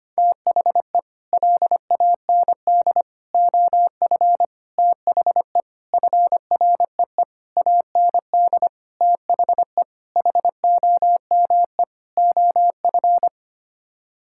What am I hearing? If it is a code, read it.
Morse code, heard as a keyed tone, about 25 wpm, 700 Hz. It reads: THE LAND OF THE FREE AND THE HOME OF
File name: Sets of 9 Words - 25wpm